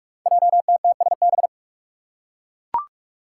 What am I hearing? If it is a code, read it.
Morse code, heard as a keyed tone, about 44 wpm, 700 Hz. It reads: JTTSB
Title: Five Challenging Letter-Number Combinations 45wpm